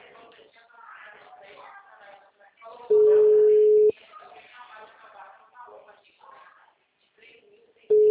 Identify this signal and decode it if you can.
Ringback tone (Europe, 425 Hz), with speech.